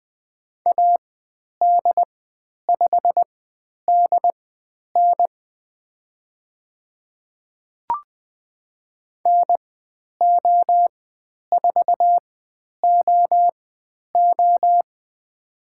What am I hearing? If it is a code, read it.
Morse code, heard as a keyed tone, about 20 wpm, spaced as for 10 wpm, 700 Hz. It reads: AD5DN NO4OO
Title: 095 Lesson 16g - Letter c - Call Signs 20wpm